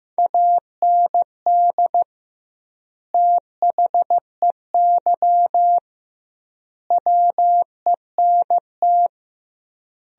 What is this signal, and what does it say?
Morse code, heard as a keyed tone, about 15 wpm, 700 Hz. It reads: AND THEY WENT
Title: 2x Word-Spacing - Sentences from Top 200 Words - 15wpm